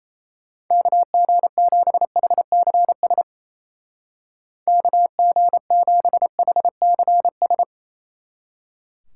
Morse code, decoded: KG7HCS KG7HCS

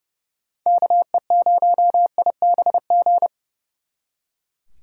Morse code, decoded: KE0IBG

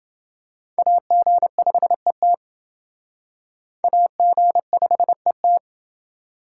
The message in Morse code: AG5ET AG5ET

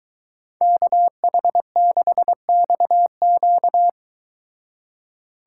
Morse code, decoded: KH6XQ

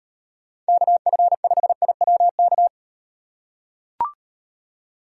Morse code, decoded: KF5IWK

